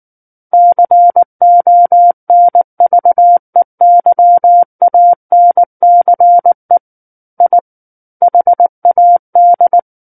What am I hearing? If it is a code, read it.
Morse code, heard as a keyed tone, about 19 wpm, 700 Hz. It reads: CONVEYANCE I HAD
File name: Book One / 10-Chapter 10 - 19 WPM